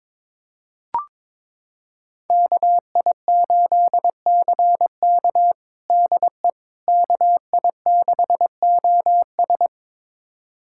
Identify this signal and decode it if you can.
Morse code, heard as a keyed tone, about 22 wpm, 700 Hz. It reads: KI8CK DE KI6OS
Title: USA Callsign DE Callsign - 5 Characters - 22wpm